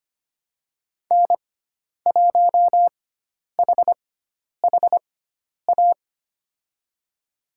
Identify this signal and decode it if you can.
Morse code, heard as a keyed tone, about 25 wpm, spaced as for 10 wpm, 700 Hz. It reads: N1HHA
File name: Lesson 16g - Letter c - Call Signs 25wpm